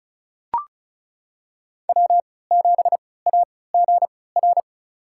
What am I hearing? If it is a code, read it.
Morse code, heard as a keyed tone, about 36 wpm, spaced as for 20 wpm, 700 Hz. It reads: W7AGR
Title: USA Call Signs - 3x Character Spacing - 35wpm